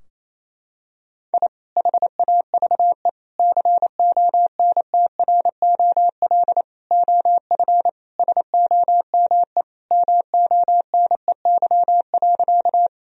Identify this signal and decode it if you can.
Morse code, heard as a keyed tone, about 28 wpm, 700 Hz. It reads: I HAVE CONTROL OF SOME MONEY.
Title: Sentences from Top 1000 Words - Review Part 3 - 28wpm